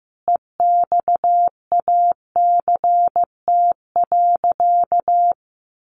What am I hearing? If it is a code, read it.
Morse code, heard as a keyed tone, about 14 wpm, 700 Hz. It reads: EXACT.
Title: Sentences from Top 1300 Words 15wpm